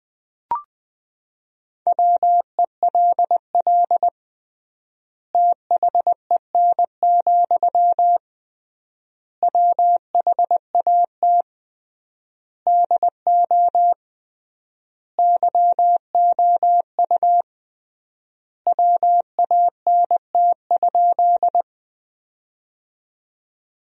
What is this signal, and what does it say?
Morse code, heard as a keyed tone, about 20 wpm, 700 Hz. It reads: WELL THEN, WHAT DO YOU WANT?
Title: 3x Word-Spacing - Sentences from Top 200 Words - 20wpm